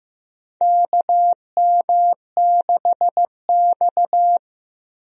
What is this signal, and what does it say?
Morse code, heard as a keyed tone, about 15 wpm, 700 Hz. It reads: KM6X
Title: USA Call Signs - 4 Characters - Repeated 3x - 15wpm